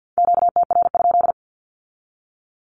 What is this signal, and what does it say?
Morse code, heard as a keyed tone, about 50 wpm, 700 Hz. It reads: QTR?
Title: Q-Codes with Spoken Meaning - 50wpm